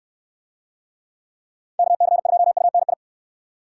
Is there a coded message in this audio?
Yes: Morse code from 1.8 to 2.9 s, heard as a keyed tone, about 66 wpm, 700 Hz. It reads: KC2UNI